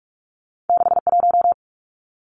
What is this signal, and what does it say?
Morse code, heard as a keyed tone, about 46 wpm, 700 Hz. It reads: 61
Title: Numbers - 2-Digit 45wpm